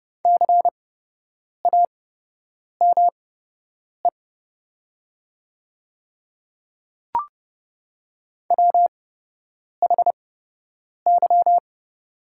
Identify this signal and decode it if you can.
Morse code, heard as a keyed tone, about 31 wpm, spaced as for 8 wpm, 700 Hz. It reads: CAME WHY
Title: 8x Character Spacing - Top 300 Words - 30wpm